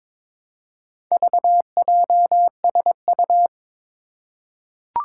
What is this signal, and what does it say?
Morse code, heard as a keyed tone, about 23 wpm, 700 Hz. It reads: VJSU
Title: Four Challenging Letter-Number Combinations 22wpm